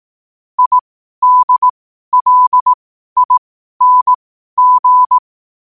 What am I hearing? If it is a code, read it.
Morse code, heard as a keyed tone, about 18 wpm, spaced as for 12 wpm, 1000 Hz. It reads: IDLING
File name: qotdfiles / 12wpm